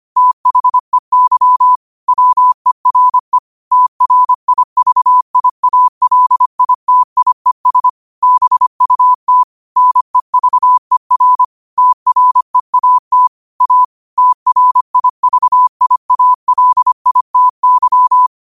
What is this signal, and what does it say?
Morse code, heard as a keyed tone, about 25 wpm, 1000 Hz. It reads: THEY WERE TRIVIALITIES BUT NEVER TREAT A TRIVIALITY
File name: 25wpm